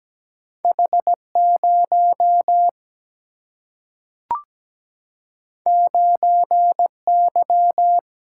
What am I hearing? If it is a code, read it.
Morse code, heard as a keyed tone, about 17 wpm, 700 Hz. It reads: H0 9Y
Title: Single-Letter Single-Number 17wpm